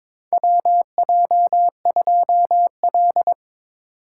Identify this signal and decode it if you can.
Morse code, heard as a keyed tone, about 21 wpm, 700 Hz. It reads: WJ2L
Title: K1USN SST QSO Practice - 22wpm